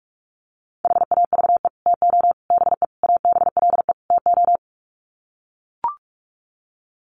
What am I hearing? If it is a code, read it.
Morse code, heard as a keyed tone, about 44 wpm, 700 Hz. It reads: HAVE TO BE ABLE TO
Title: Sets of 5 Words 45wpm